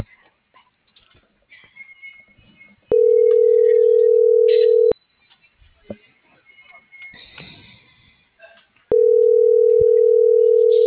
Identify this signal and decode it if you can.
Ringback tone (North America), 440 + 480 Hz.